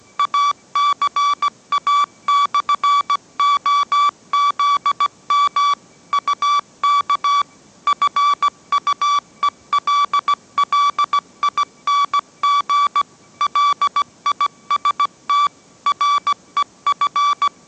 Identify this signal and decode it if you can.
Morse code, heard as a keyed tone, about 17 wpm, 1150 Hz. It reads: ACA/OZM UK FUELLING LIST REF
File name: mrx / mrx080 18wpm